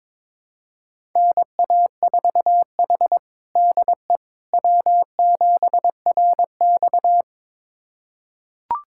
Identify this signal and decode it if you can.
Morse code, heard as a keyed tone, about 22 wpm, 700 Hz. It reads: NA4H DE W7RX
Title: USA Callsign DE Callsign - 4 Characters - 22wpm